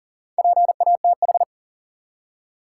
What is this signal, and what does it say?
Morse code, heard as a keyed tone, about 37 wpm, 700 Hz. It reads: PATH
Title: Four Letter Words - 40wpm